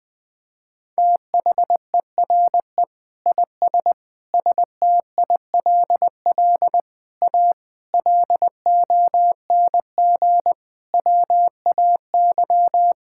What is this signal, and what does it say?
Morse code, heard as a keyed tone, about 20 wpm, 700 Hz. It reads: THERE IS STILL A LONG WAY
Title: Sentences with 8-Word Phrases - 20wpm